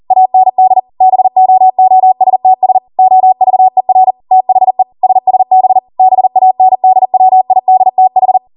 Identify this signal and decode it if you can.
Morse code, heard as a keyed tone, about 40 wpm, 750 Hz. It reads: AND BOOSTS OVER THE SSB BANDWIDTH